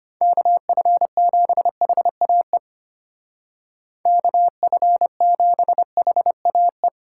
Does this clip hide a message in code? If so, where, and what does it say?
Yes: Morse code from 0.2 to 6.9 s, heard as a keyed tone, about 33 wpm, 700 Hz. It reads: KF7HAE KF7HAE